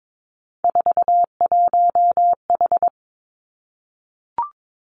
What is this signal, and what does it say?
Morse code, heard as a keyed tone, about 22 wpm, 700 Hz. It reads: 41H
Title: Three Challenging Letter-Number Combinations 22wpm